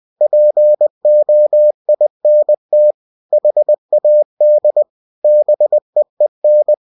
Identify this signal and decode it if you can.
Morse code, heard as a keyed tone, about 20 wpm, 600 Hz. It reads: POINT HAD BEEN
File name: cw-books / winnie-the-pooh / winnie-20wpm-page-0035